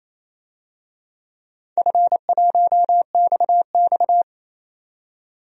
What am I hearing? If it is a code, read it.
Morse code, heard as a keyed tone, about 28 wpm, 700 Hz. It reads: F1XX
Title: DXCC-100 Call Signs - 4 Characters - 28wpm